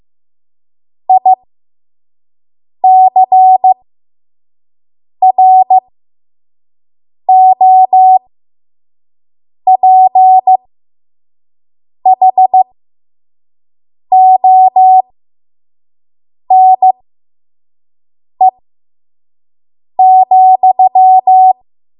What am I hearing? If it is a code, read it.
Morse code, heard as a keyed tone, about 15 wpm, spaced as for 5 wpm, 750 Hz. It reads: ICROPHONE,
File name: Morse / 05WPM